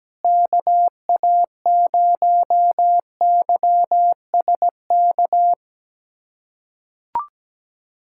Morse code, heard as a keyed tone, about 17 wpm, 700 Hz. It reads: KA0YSK
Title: USA Callsign DE Callsign 17wpm